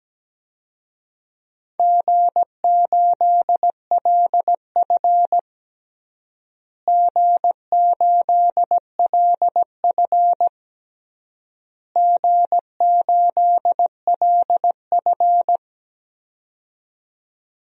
Morse code, heard as a keyed tone, about 17 wpm, 700 Hz. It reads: G8LF G8LF G8LF
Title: DXCC-100 Call Signs - 4 Characters - Repeated 3x - 17wpm